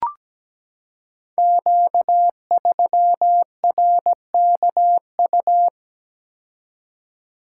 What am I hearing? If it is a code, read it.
Morse code, heard as a keyed tone, about 17 wpm, 700 Hz. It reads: Q3RKU
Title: Code Groups - 5 Letter:Numbers - 17wpm